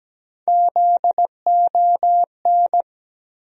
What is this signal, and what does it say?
Morse code, heard as a keyed tone, about 17 wpm, 700 Hz. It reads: ZON
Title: USA Call Sign Postfixes - 3 Letters 17wpm